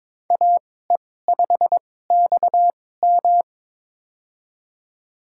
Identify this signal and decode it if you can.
Morse code, heard as a keyed tone, about 22 wpm, spaced as for 16 wpm, 700 Hz. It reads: AE5XM
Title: USA Call Signs - 5 Characters - 2x Character Spacing - 22wpm